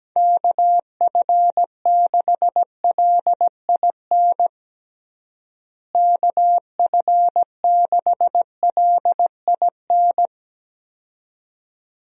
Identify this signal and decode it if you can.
Morse code, heard as a keyed tone, about 17 wpm, 700 Hz. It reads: KF6LIN KF6LIN
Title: USA Call Signs - 6 Characters - Repeated 2x - 17wpm